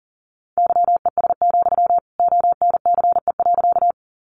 Morse code, heard as a keyed tone, about 40 wpm, 700 Hz. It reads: YES, ONCE.
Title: Sentences from Top 500 Words 40wpm